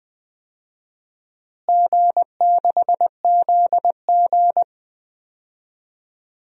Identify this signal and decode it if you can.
Morse code, heard as a keyed tone, about 21 wpm, 700 Hz. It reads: G6ZG
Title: DXCC-100 Call Signs - 4 Characters - Repeated 3x - 20wpm